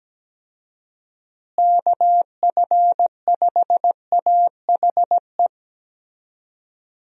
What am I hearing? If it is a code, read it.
Morse code, heard as a keyed tone, about 17 wpm, 700 Hz. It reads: KF5AHE
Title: USA Call Signs - 6 Characters - Repeated 2x - 17wpm